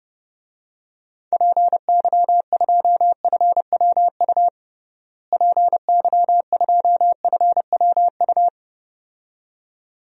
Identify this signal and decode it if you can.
Morse code, heard as a keyed tone, about 30 wpm, 700 Hz. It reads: PY2FWU PY2FWU
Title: DXCC-100 Call Signs - 6 Characters - Repeated 2x - 30wpm